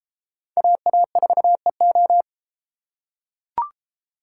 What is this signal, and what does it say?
Morse code, heard as a keyed tone, about 33 wpm, 700 Hz. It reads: AA4EO